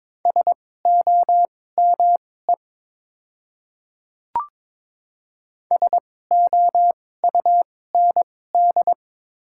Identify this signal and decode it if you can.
Morse code, heard as a keyed tone, about 22 wpm, spaced as for 16 wpm, 700 Hz. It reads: SOME SOUND
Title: 2x Character Spacing - Top 100 Words - 22wpm